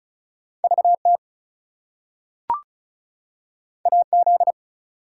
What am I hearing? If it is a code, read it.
Morse code, heard as a keyed tone, about 36 wpm, 700 Hz. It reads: VT AZ
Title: US States and Canadian Provinces 35wpm